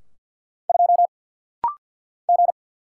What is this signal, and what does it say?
Morse code, heard as a keyed tone, about 50 wpm, 700 Hz. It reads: 2 G
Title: Single Letter-Number - Rapid-Fire 50wpm